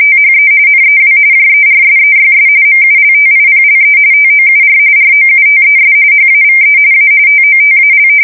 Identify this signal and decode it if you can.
RTTY, about 45.45 baud. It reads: THE NEW JERSEYS MULTIPLE TOUR ROUTES HAVE A TOTAL
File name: fsk170 / battleshipnj